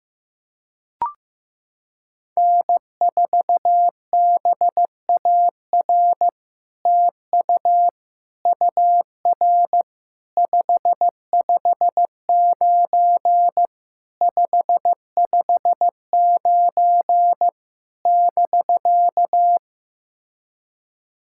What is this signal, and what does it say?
Morse code, heard as a keyed tone, about 15 wpm, 700 Hz. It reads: N4BAR TU UR 559 559 <BK>
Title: SOTA QSO Practice 15wpm